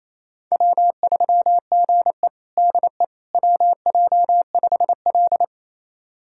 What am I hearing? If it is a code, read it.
Morse code, heard as a keyed tone, about 28 wpm, 700 Hz. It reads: W3GE DE WJ5L
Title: USA Callsign DE Callsign - 4 Characters - 28wpm